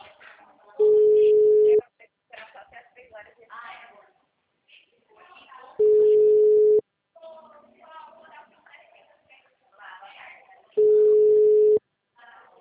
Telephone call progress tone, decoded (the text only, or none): ringback tone